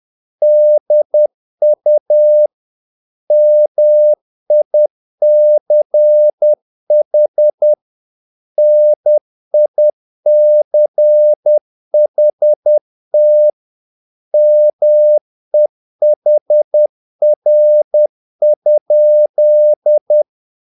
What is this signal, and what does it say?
Morse code, heard as a keyed tone, about 10 wpm, 600 Hz. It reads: DU MICH NICHT MEHR?
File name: dk5ke / Humor-50BpM